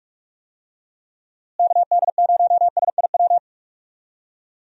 Morse code, heard as a keyed tone, about 41 wpm, 700 Hz. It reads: KD0SIW
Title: USA Call Signs - 6 Characters - Repeated 2x - 45wpm